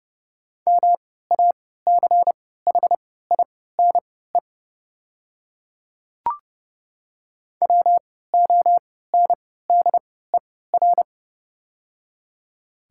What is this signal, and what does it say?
Morse code, heard as a keyed tone, about 30 wpm, spaced as for 17 wpm, 700 Hz. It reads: MACHINE WONDER
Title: 3x Character Spacing - Top 500 Words - 30wpm